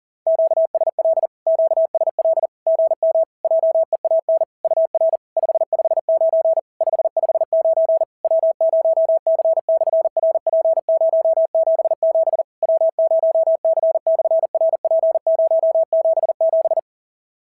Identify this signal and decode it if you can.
Morse code, heard as a keyed tone, about 40 wpm, 650 Hz. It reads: QSL QSL GM JEAN UR 559 559 W0C/RP077 W0C/RP077
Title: SOTA QSO Practice 40wpm